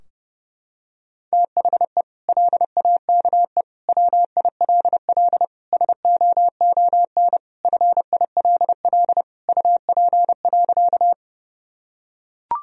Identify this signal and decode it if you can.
Morse code, heard as a keyed tone, about 32 wpm, 700 Hz. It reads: THE LAKE WILL SOON FILL UP.
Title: Sentences from Top 1000 Words - Review Part 3 - 30wpm